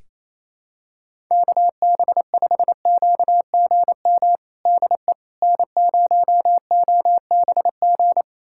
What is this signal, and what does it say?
Morse code, heard as a keyed tone, about 27 wpm, 700 Hz. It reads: KB5QGM DE N0OBG
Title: USA Callsign DE Callsign 28wpm